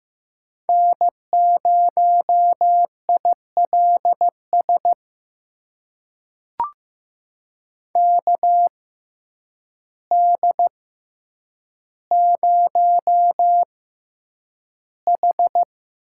Morse code, heard as a keyed tone, about 15 wpm, spaced as for 8 wpm, 700 Hz. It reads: N0ILS KD0H
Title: USA Call Signs - 5 Characters - 6x Character Spacing - 15wpm